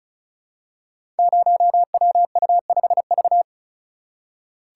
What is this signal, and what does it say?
Morse code, heard as a keyed tone, about 36 wpm, 700 Hz. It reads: 0WU5V
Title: Code Groups - 5 Letter:Numbers - 35wpm